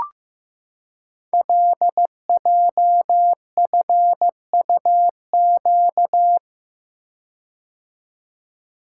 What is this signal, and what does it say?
Morse code, heard as a keyed tone, about 15 wpm, 700 Hz. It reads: LJFUQ
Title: Code Groups - 5 Letters - 15wpm